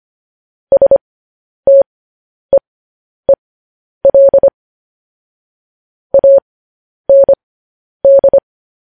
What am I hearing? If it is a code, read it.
Morse code, heard as a keyed tone, about 25 wpm, spaced as for 10 wpm, 550 Hz. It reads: STEEL AND
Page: news headlines in morse code at 10 wpm